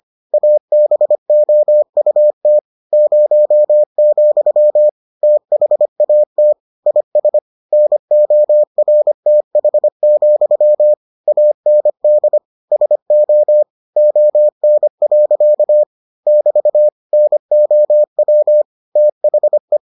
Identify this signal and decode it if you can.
Morse code, heard as a keyed tone, about 25 wpm, 600 Hz. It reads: ABOUT 0, THAT IS NORTH, AND SO ON. <BT> NOW THE